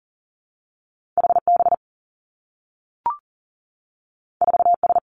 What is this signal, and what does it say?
Morse code, heard as a keyed tone, about 40 wpm, 700 Hz. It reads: H B 4 S
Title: Two Challenging Letter-Number Combinations 40wpm